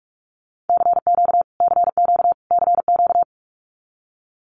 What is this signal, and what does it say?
Morse code, heard as a keyed tone, about 44 wpm, 700 Hz. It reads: CQ CQ CQ
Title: Abbreviations Repeated 3x - 45wpm